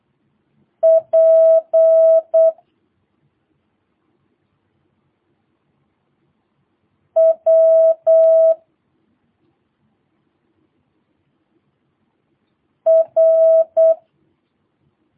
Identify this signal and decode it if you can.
Morse code, heard as a keyed tone, about 8 wpm, 650 Hz. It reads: P W R
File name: jh8cbh / 2022cw